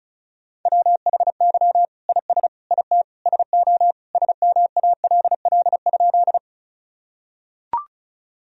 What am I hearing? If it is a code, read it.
Morse code, heard as a keyed tone, about 36 wpm, 700 Hz. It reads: WHY IS IT SO SMALL?
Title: Sentences from Top 1000 Words - Review Part 1 - 35wpm